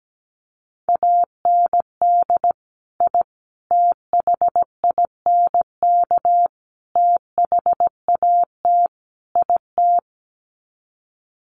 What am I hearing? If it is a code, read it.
Morse code, heard as a keyed tone, about 17 wpm, 700 Hz. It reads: AND I THINK THAT IT
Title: Sets of 5 Words Encore2 17wpm